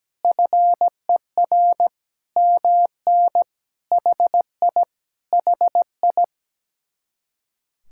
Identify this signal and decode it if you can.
Morse code, heard as a keyed tone, about 17 wpm, 700 Hz. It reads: FER MN HI HI
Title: Speed-Racing Common QSO Phrases - 17wpm